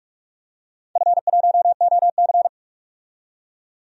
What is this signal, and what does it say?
Morse code, heard as a keyed tone, about 42 wpm, 700 Hz. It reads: F1OC
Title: DXCC-100 Call Signs - 4 Characters - 45wpm